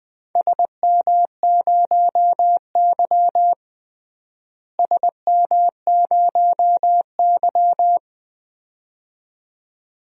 Morse code, heard as a keyed tone, about 20 wpm, 700 Hz. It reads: SM0Y SM0Y
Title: DXCC-100 Call Signs - 4 Characters - Repeated 2x - 20wpm